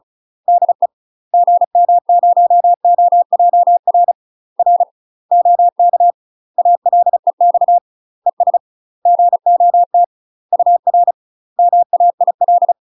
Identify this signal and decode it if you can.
Morse code, heard as a keyed tone, about 34 wpm, 700 Hz. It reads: DE GM0OJR R OK ALEX ES GOT UR MAIL
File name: UK MASTER QSOs TRAINING / 35WPM / UK MASTER QSOs 0042